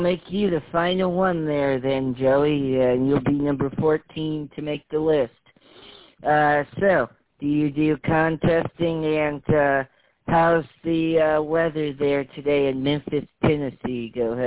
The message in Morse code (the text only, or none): none